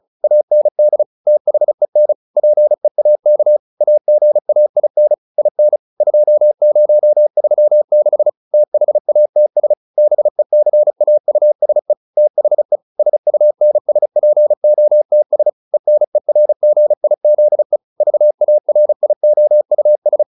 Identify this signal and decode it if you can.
Morse code, heard as a keyed tone, about 34 wpm, 600 Hz. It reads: AND THEN PEAK AGAIN IN 2036 THATS BECAUSE THE SUNSPOTS ENERGIZE VARIOUS